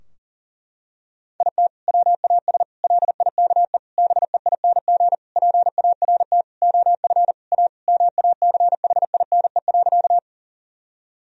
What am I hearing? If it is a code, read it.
Morse code, heard as a keyed tone, about 40 wpm, 700 Hz. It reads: IT WAS LIKE BEING PART OF A MACHINE.
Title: Sentences from Top 1000 Words - Review Part 3 - 40wpm